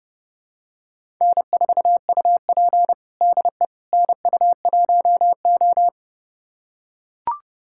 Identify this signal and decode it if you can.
Morse code, heard as a keyed tone, about 30 wpm, 700 Hz. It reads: N4UP DE NU1O
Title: USA Callsign DE Callsign - 4 Characters - 30wpm